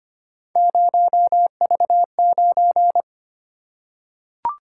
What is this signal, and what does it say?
Morse code, heard as a keyed tone, about 25 wpm, 700 Hz. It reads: 0V9
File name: Three Letter-Number Combinations - 25wpm